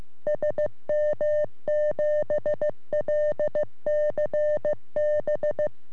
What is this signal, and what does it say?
Morse code, heard as a keyed tone, about 15 wpm, 600 Hz. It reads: SM7LCB
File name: sm7lcb / g3xdy 3cm